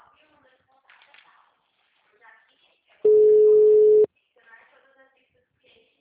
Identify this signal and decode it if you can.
Ringback tone (Europe, 425 Hz), with speech.